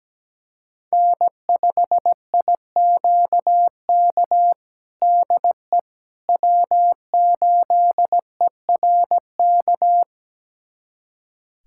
Morse code, heard as a keyed tone, about 17 wpm, 700 Hz. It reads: N5IQK DE W8ERK
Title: USA Callsign DE Callsign - 5 Characters - 17wpm